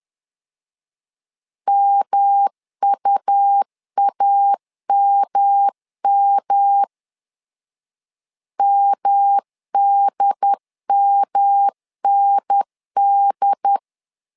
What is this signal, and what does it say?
Morse code, heard as a keyed tone, about 11 wpm, 800 Hz. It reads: MUAMM MDMND
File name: LEZIONI / 3 MWRO / 3 M 10wpm 1828